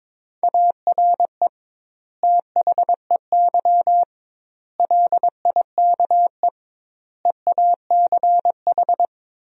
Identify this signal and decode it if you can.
Morse code, heard as a keyed tone, about 22 wpm, 700 Hz. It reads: ARE THEY LIKE EACH
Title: 2x Word-Spacing - Sentences from Top 200 Words - 22wpm